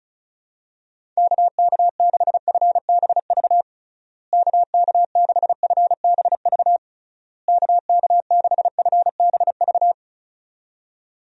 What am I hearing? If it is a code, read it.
Morse code, heard as a keyed tone, about 36 wpm, 700 Hz. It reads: KK6FBV KK6FBV KK6FBV
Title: USA Call Signs - 6 Characters - Repeated 3x - 35wpm